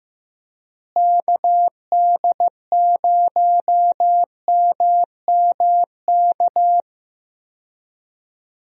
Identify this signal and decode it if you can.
Morse code, heard as a keyed tone, about 15 wpm, 700 Hz. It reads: KD0MMK
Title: USA Call Signs - 6 Characters - Repeated 3x - 15wpm